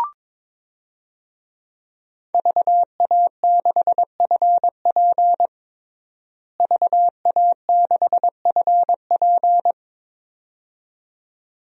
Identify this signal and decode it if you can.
Morse code, heard as a keyed tone, about 22 wpm, 700 Hz. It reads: VA6FP VA6FP
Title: DXCC-100 Call Signs - 5 Characters - Repeated 2x - 22wpm